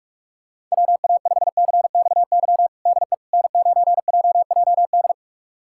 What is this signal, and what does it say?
Morse code, heard as a keyed tone, about 46 wpm, 700 Hz. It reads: WA5CXY DE N9JJD
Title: USA Callsign DE Callsign 45wpm